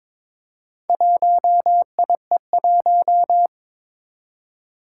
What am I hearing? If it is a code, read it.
Morse code, heard as a keyed tone, about 22 wpm, 700 Hz. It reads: 1IE1
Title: Four Challenging Letter-Number Combinations 22wpm